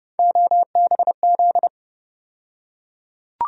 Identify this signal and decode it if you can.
Morse code, heard as a keyed tone, about 32 wpm, 700 Hz. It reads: OBZ
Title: USA Call Sign Postfixes - 3 Letters 30wpm